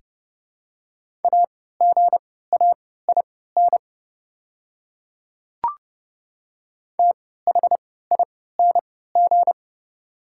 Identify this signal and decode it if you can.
Morse code, heard as a keyed tone, about 30 wpm, spaced as for 17 wpm, 700 Hz. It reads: AGAIN THING
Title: 3x Character Spacing - Top 200 Words - 30wpm